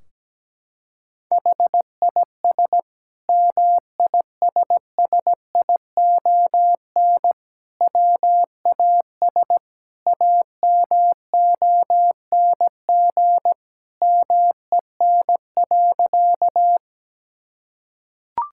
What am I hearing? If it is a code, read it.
Morse code, heard as a keyed tone, about 17 wpm, 700 Hz. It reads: HIS MISSION WAS AMONG MEN.
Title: Sentences from Top 1600 Words 17wpm